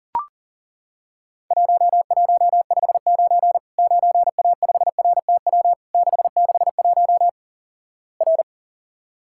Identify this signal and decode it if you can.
Morse code, heard as a keyed tone, about 37 wpm, 700 Hz. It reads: 1159 9A5RTW 661 R
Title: Worked All Europe DX Contest - QTC Exchange - 40wpm